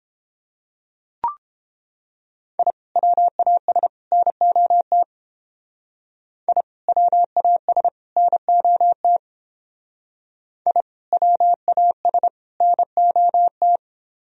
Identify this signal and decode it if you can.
Morse code, heard as a keyed tone, about 29 wpm, 700 Hz. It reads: I WAS NOT I WAS NOT I WAS NOT